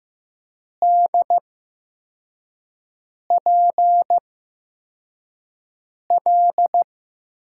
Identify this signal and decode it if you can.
Morse code, heard as a keyed tone, about 15 wpm, 700 Hz. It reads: D P L
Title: USA Call Signs - 5 Characters - 8x Character Spacing - 15wpm